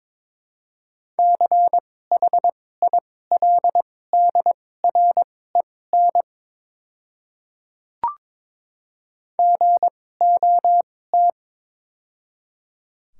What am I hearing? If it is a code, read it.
Morse code, heard as a keyed tone, about 22 wpm, spaced as for 16 wpm, 700 Hz. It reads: CHILDREN GOT
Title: 2x Character Spacing - Top 300 Words - 22wpm